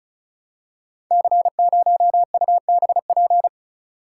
Morse code, heard as a keyed tone, about 36 wpm, 700 Hz. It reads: C0UBP
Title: Code Groups - 5 Letter:Numbers - 35wpm